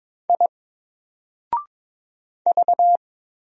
Morse code, heard as a keyed tone, about 22 wpm, 700 Hz. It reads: I V
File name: Single-Letter-Number-22wpm